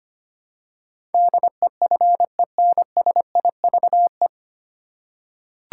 Morse code, heard as a keyed tone, about 25 wpm, 700 Hz. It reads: DEFENSIVE
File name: practice / 25wpm / 9 Letter Words - First 3 Letters Given - 25wpm